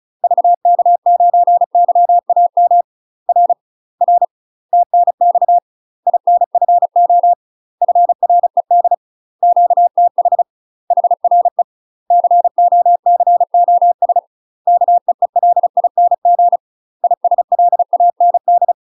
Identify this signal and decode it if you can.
Morse code, heard as a keyed tone, about 34 wpm, 700 Hz. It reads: VK9YAM R R TNX INFO FRED QTH HRE COCOS KEELING ISLAND
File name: UK MASTER QSOs TRAINING / 35WPM / UK MASTER QSOs 0023